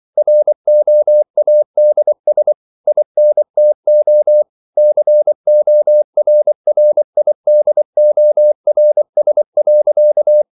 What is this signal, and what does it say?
Morse code, heard as a keyed tone, about 24 wpm, 600 Hz. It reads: ROADS INTO CORRIDORS.